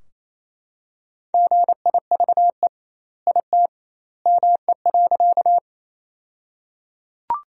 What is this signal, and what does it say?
Morse code, heard as a keyed tone, about 28 wpm, 700 Hz. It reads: GIVE IT ME.
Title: 2x Word-Spacing - Sentences from Top 200 Words - 28wpm